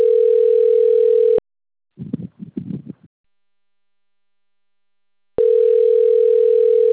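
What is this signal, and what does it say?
Ringback tone (North America), 440 + 480 Hz.
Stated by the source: Outgoing telephone call recorded on a phone system labelled Bebedouro 1.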